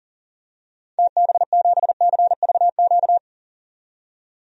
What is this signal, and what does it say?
Morse code, heard as a keyed tone, about 40 wpm, 700 Hz. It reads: TB7CVQ
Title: DXCC-100 Call Signs - 6 Characters - 40wpm